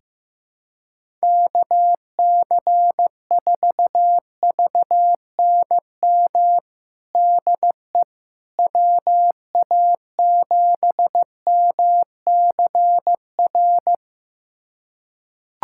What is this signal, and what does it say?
Morse code, heard as a keyed tone, about 15 wpm, 700 Hz. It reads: KC4VNM DE WA7MCR
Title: USA Callsign DE Callsign - 6 Characters - 15wpm